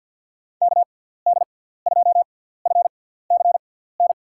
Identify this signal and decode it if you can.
Morse code, heard as a keyed tone, about 52 wpm, spaced as for 17 wpm, 700 Hz. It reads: KD2FCN
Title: USA Call Signs - 6x Character Spacing - 50wpm